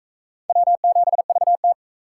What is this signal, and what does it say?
Morse code, heard as a keyed tone, about 37 wpm, 700 Hz. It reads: W7VT